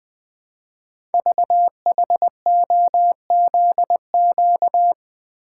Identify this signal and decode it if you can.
Morse code, heard as a keyed tone, about 20 wpm, 700 Hz. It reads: VHOZQ
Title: Code Groups - 5 Letters - 20wpm